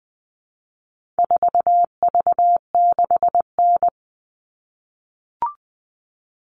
Morse code, heard as a keyed tone, about 20 wpm, 700 Hz. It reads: 4V6N
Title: All Cut Numbers - 4-Digit 20wpm